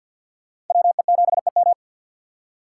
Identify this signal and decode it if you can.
Morse code, heard as a keyed tone, about 47 wpm, 700 Hz. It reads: WE7EM
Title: USA Call Signs - 4x Character Spacing - 50wpm